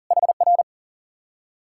Morse code, heard as a keyed tone, about 49 wpm, 700 Hz. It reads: HR
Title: Speed-Racing Common QSO Phrases - 40wpm